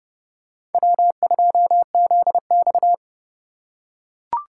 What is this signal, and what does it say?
Morse code, heard as a keyed tone, about 32 wpm, 700 Hz. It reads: W2ZX
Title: USA Call Signs - 2x Character Spacing - 30wpm